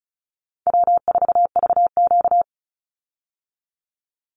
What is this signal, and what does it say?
Morse code, heard as a keyed tone, about 34 wpm, 700 Hz. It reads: W4VQ
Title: Call Signs Repeated 2x - 35wpm